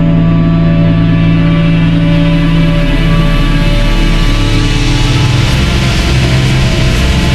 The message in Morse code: none